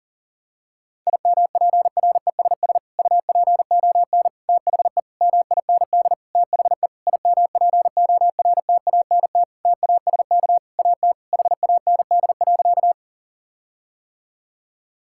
Morse code, heard as a keyed tone, about 40 wpm, 700 Hz. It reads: IMPRESS UPON THE MIND THE IMPORTANT TASK AT HAND.
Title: Sentences from Top 1600 Words 40wpm